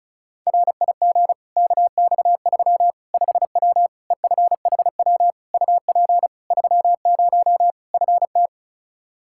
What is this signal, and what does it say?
Morse code, heard as a keyed tone, about 36 wpm, 700 Hz. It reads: RIG KX3 5W EFHW UP 30 FT
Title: Practice QSO - Exchange 2 - Info - 35wpm